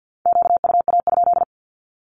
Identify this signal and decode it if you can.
Morse code, heard as a keyed tone, about 48 wpm, 700 Hz. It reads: QUA?
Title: Q-Codes Repeated 3x 50wpm